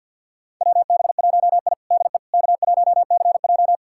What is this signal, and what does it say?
Morse code, heard as a keyed tone, about 52 wpm, 700 Hz. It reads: WB1I DE K1CJ
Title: USA Callsign DE Callsign - 4 Characters - 50wpm